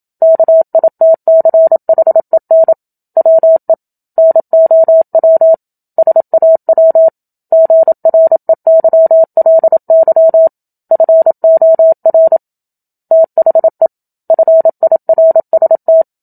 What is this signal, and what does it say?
Morse code, heard as a keyed tone, about 27 wpm, 650 Hz. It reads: KITCHEN WE NOW SAW GREYLY FOR THE FIRST
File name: Book Two / 01-Chapter 1 - 27 WPM